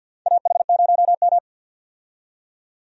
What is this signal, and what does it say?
Morse code, heard as a keyed tone, about 51 wpm, 700 Hz. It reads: AH0M
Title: USA Call Signs - 4 Characters - 50wpm